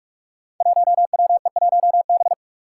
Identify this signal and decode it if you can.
Morse code, heard as a keyed tone, about 44 wpm, 700 Hz. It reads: 1WE1B
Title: Five Challenging Letter-Number Combinations 45wpm